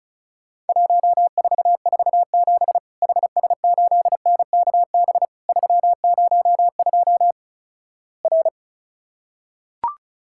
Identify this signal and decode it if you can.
Morse code, heard as a keyed tone, about 36 wpm, 700 Hz. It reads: 1447 HS8NKB 302 R
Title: Worked All Europe DX Contest - QTC Exchange - 35wpm